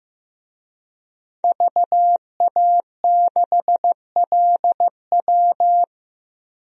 Morse code, heard as a keyed tone, about 15 wpm, 700 Hz. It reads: VA6LW
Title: DXCC-100 Call Signs - 5 Characters - 15wpm